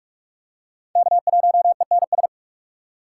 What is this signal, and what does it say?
Morse code, heard as a keyed tone, about 37 wpm, 700 Hz. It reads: K1ENS
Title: Speed Racing - USA 1x3 Call Signs 30wpm